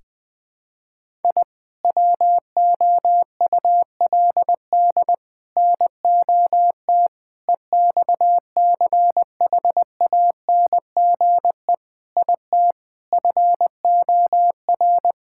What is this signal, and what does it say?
Morse code, heard as a keyed tone, about 20 wpm, 700 Hz. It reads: I WOULD NOT EXCHANGE IT FOR
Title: Sentences from Top 1000 Words - Review Part 6 - 20wpm